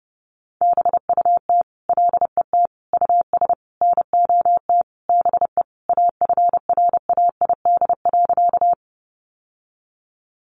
Morse code, heard as a keyed tone, about 28 wpm, 700 Hz. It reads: BUT LET US NOT BE AFRAID.
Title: Sentences from Top 1500 Words 30wpm